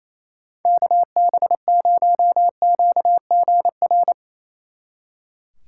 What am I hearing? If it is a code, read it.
Morse code, heard as a keyed tone, about 34 wpm, 700 Hz. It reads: KB0QGR
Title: Speed Racing - USA Call Signs 28wpm